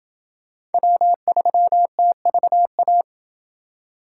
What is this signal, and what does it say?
Morse code, heard as a keyed tone, about 28 wpm, 700 Hz. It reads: W3TVA